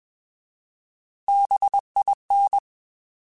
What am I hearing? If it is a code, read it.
Morse code, heard as a keyed tone, about 21 wpm, 800 Hz. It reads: BIN
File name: Alphabetical / Alphabetical mp3s / Bs @20